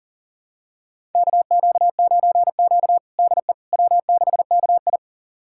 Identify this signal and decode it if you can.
Morse code, heard as a keyed tone, about 40 wpm, 700 Hz. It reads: KQ9Q DE W6KI
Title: USA Callsign DE Callsign - 4 Characters - 40wpm